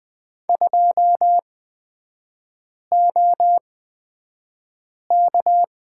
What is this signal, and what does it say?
Morse code, heard as a keyed tone, about 20 wpm, spaced as for 5 wpm, 700 Hz. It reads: 2OK
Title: 250 Lesson 40e - BK - Three Characters 20wpm